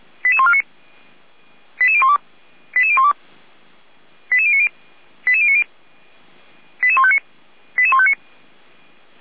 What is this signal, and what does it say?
Five-tone selective call (ZVEI), heard as quick runs of five tones, tones of about 70 ms. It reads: 88149 88012 88012 88090 88090 88159 88159